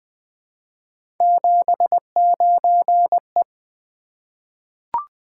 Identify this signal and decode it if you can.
Morse code, heard as a keyed tone, about 20 wpm, 700 Hz. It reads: 79E
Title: Three Letter-Number Combinations - 20wpm